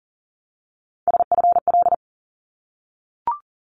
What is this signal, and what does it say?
Morse code, heard as a keyed tone, about 40 wpm, 700 Hz. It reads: SFL
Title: ARRL Sections 40wpm